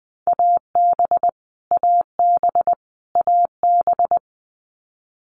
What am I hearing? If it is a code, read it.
Morse code, heard as a keyed tone, about 20 wpm, 700 Hz. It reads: AB AB AB
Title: Canadian Provinces and Territories - Repeated 3x - 20wpm